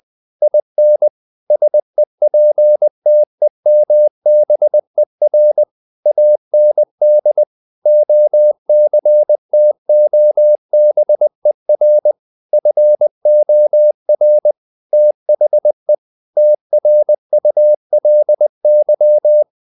Morse code, heard as a keyed tone, about 20 wpm, 600 Hz. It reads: IN SEPTEMBER AND OCTOBER FOR THE TRULY
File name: the old man and the sea / the old man and the sea part 1 20wpm